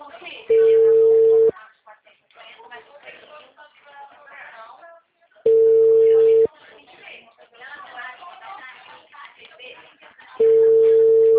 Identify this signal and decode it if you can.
Ringback tone (China, 450 Hz), with speech.
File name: out- -2005- - - .76275